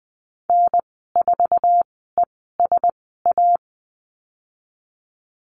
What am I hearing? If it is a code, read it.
Morse code, heard as a keyed tone, about 20 wpm, spaced as for 14 wpm, 700 Hz. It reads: N4ESA
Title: 082 Lesson 14g - Number 2 - Call Signs 20wpm